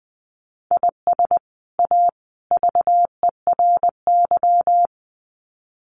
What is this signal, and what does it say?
Morse code, heard as a keyed tone, about 20 wpm, 700 Hz. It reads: IS A VERY
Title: Sets of 3 Words 20wpm